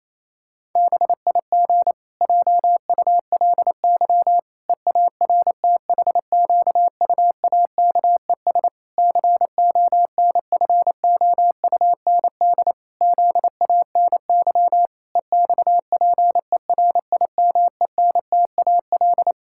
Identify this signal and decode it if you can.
Morse code, heard as a keyed tone, about 28 wpm, 700 Hz. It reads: BIG JULY EARTHQUAKES CONFOUND ZANY EXPERIMENTAL
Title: Pangram Challenge v2 - 28wpm